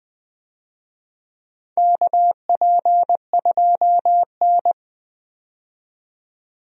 Morse code, heard as a keyed tone, about 20 wpm, 700 Hz. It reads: KP2N
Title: DXCC-100 Call Signs - 4 Characters - 20wpm